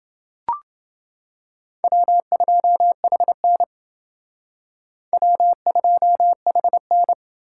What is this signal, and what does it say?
Morse code, heard as a keyed tone, about 28 wpm, 700 Hz. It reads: W2HN W2HN